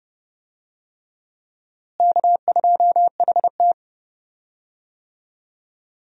Morse code, heard as a keyed tone, about 30 wpm, 700 Hz. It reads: K2HT
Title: USA Call Signs - 4 Characters - Repeated 3x - 30wpm